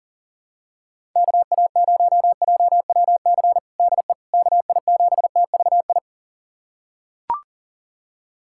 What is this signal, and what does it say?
Morse code, heard as a keyed tone, about 40 wpm, 700 Hz. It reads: KA0JWC DE KI7TVI
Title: USA Callsign DE Callsign - 6 Characters - 40wpm